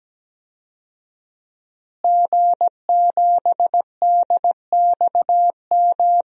Morse code, heard as a keyed tone, about 17 wpm, 700 Hz. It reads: G7DXM
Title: DXCC-100 Call Signs - 5 Characters - Repeated 2x - 17wpm